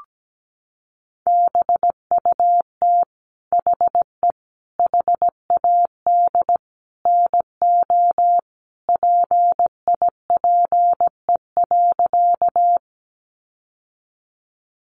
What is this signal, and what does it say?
Morse code, heard as a keyed tone, about 17 wpm, 700 Hz. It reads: BUT HE HAD NO PIPE.
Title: Sentences from Top 1500 Words 17wpm